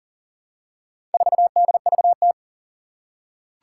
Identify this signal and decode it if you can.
Morse code, heard as a keyed tone, about 39 wpm, 700 Hz. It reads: 4DVT
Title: Four Challenging Letter-Number Combinations 40wpm